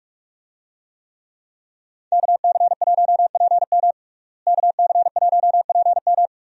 Morse code, heard as a keyed tone, about 44 wpm, 700 Hz. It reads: KC1PM KC1PM
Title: USA Call Signs - 5 Characters - Repeated 2x - 45wpm